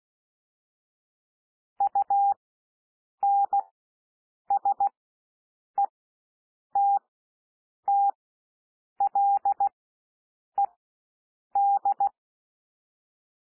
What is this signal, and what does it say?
Morse code, heard as a keyed tone, about 16 wpm, spaced as for 7 wpm, 800 Hz. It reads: UNSETTLED
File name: SOUND WORDS 16 wpm